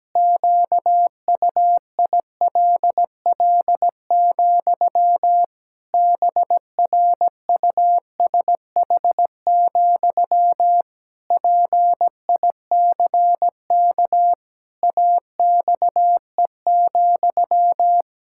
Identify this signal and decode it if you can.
Morse code, heard as a keyed tone, about 17 wpm, 700 Hz. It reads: QUILL, BRUSH, PICK AXE,
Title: Pangram Challenge v2 - 17wpm